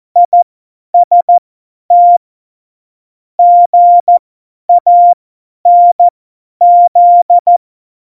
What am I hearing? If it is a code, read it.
Morse code, heard as a keyed tone, about 14 wpm, spaced as for 10 wpm, 700 Hz. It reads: IST GANZ
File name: Morsekurs / Witze / Witz 05 Mai 2020, 50 F70